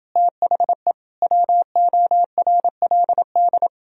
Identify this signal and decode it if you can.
Morse code, heard as a keyed tone, about 30 wpm, 700 Hz. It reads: THE WORLD